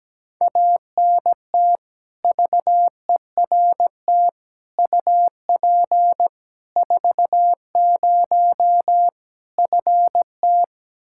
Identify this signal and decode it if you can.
Morse code, heard as a keyed tone, about 17 wpm, 700 Hz. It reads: ANT VERT UP 40 FT
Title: Practice QSO - Exchange 2 - Info - 17wpm